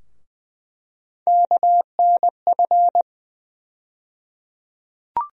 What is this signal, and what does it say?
Morse code, heard as a keyed tone, about 20 wpm, 700 Hz. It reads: KNF
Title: USA Call Sign Postfixes - 3 Letters 20wpm